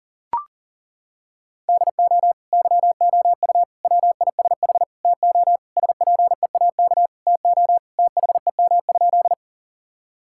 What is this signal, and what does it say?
Morse code, heard as a keyed tone, about 40 wpm, 700 Hz. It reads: DO YOU WISH TO SPEAK TO THEM?
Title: Sentences from Top 1600 Words 40wpm